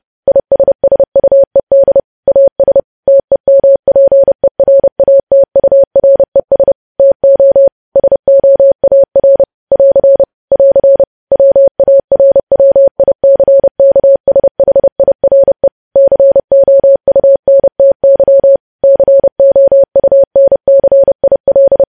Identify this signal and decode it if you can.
Morse code, heard as a keyed tone, about 30 wpm, 550 Hz. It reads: ISSUED AS TEMPERATURES TO SOAR <AR> <AR> WARWICKSHIRE COUNTY COUNCIL
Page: News Headlines in Morse Code at 30 WPM